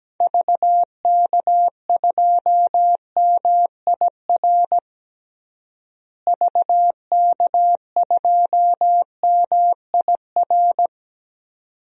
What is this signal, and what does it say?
Morse code, heard as a keyed tone, about 17 wpm, 700 Hz. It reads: VK2MIR VK2MIR
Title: DXCC-100 Call Signs - 6 Characters - Repeated 2x - 17wpm